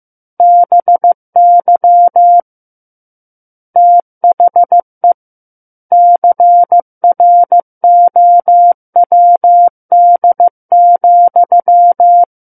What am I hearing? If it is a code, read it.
Morse code, heard as a keyed tone, about 15 wpm, 700 Hz. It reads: BY THE CROWD,
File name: Chapter 6 - 15 WPM